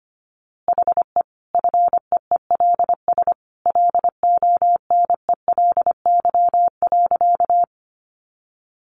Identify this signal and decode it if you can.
Morse code, heard as a keyed tone, about 25 wpm, 700 Hz. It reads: HE FEELS LONELY.
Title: Easy Phrases 25wpm